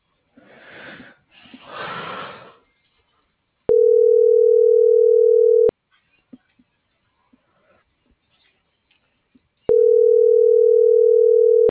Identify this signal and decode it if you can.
Ringback tone (North America), 440 + 480 Hz.
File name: CENTRAL SAGOT / out danae.bsagot - .18716